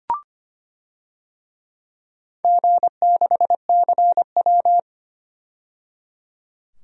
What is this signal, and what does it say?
Morse code, heard as a keyed tone, about 25 wpm, 690 Hz. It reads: G6CW
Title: DXCC-100 Call Signs - 4 Characters - 25wpm